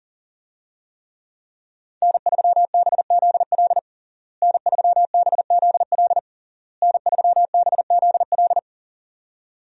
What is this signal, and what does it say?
Morse code, heard as a keyed tone, about 40 wpm, 700 Hz. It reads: N3BZL N3BZL N3BZL
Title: USA Call Signs - 5 Characters - Repeated 3x - 40wpm